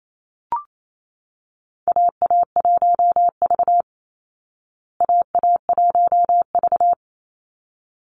Morse code, heard as a keyed tone, about 28 wpm, 700 Hz. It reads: AA1V AA1V
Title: Call Signs Repeated 3x - 28wpm